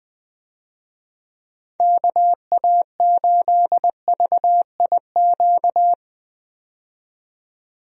Morse code, heard as a keyed tone, about 20 wpm, 700 Hz. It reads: KA8VIQ
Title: USA Call Signs - 6 Characters - 20wpm